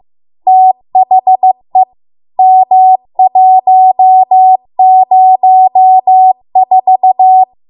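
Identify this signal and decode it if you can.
Morse code, heard as a keyed tone, about 15 wpm, 750 Hz. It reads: THE M104
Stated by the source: Morse code practice at 15 wpm from the national amateur radio association's code archive.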